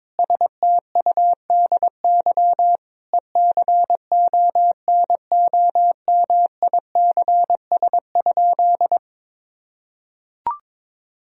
Morse code, heard as a keyed tone, about 22 wpm, 700 Hz. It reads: STUDY ECONOMICS?
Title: Sentences from Top 1000 Words - Review Part 3 - 22wpm